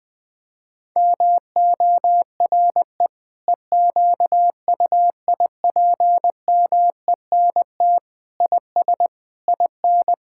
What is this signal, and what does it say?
Morse code, heard as a keyed tone, about 21 wpm, 700 Hz. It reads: MORE EQUIPMENT IS IN
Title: Sentences from Top 1000 Words - Review Part 5 - 20wpm